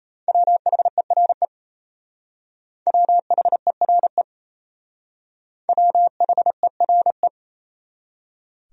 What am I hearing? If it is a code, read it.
Morse code, heard as a keyed tone, about 38 wpm, 700 Hz. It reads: WHERE WHERE WHERE